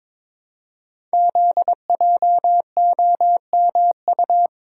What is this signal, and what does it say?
Morse code, heard as a keyed tone, about 22 wpm, 700 Hz. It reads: ZJOMU
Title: Code Groups - 5 Letters - 22wpm